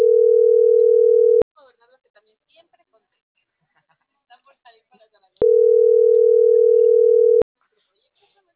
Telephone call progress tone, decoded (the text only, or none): ringback tone